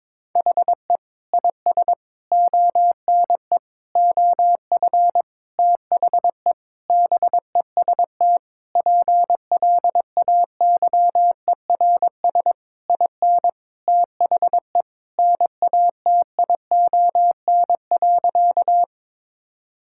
Morse code, heard as a keyed tone, about 22 wpm, 700 Hz. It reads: HE IS ONE OF THE BEST PLAYERS IN THE NATION.
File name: Sentences with 7-Word Phrases - 22wpm